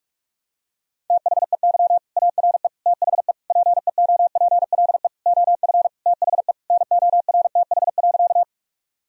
Morse code, heard as a keyed tone, about 46 wpm, 700 Hz. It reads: THEY ARE THE PEOPLE OF THE NORTH.
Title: Sentences from Top 1000 Words - Review Part 1 - 45wpm